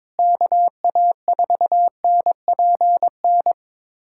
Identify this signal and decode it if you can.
Morse code, heard as a keyed tone, about 22 wpm, 700 Hz. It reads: KA4NPN
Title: USA Callsign DE Callsign - 6 Characters - 22wpm